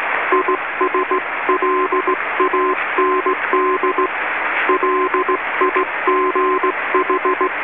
Morse code, heard as a keyed tone, about 17 wpm, 1040 Hz, with radio noise. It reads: ISLAND LIGH
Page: All of these recordings were made by the author onboard the New Zealand inter island rail ferry ARAHURA / ZMBS in November 1993 when I was Radio Officer.